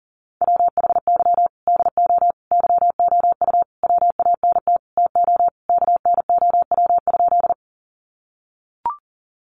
Morse code, heard as a keyed tone, about 40 wpm, 700 Hz. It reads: WHY DO YOU WANT TO KNOW?
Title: Sentences from Top 200 Words 40wpm